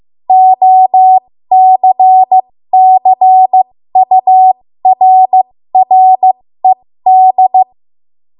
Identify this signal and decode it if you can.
Morse code, heard as a keyed tone, about 15 wpm, spaced as for 13 wpm, 750 Hz. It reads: OCCURRED
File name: Morse / 13WPM